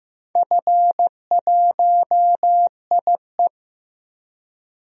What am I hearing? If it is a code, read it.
Morse code, heard as a keyed tone, about 15 wpm, 700 Hz. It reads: F1IE
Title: DXCC-100 Call Signs - 4 Characters - Repeated 3x - 15wpm